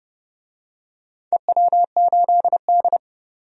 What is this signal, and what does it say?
Morse code, heard as a keyed tone, about 31 wpm, 700 Hz. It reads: EW8D
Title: DXCC-100 Call Signs - 4 Characters - 30wpm